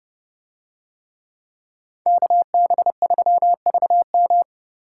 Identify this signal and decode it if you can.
Morse code, heard as a keyed tone, about 28 wpm, 700 Hz. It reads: KB3VM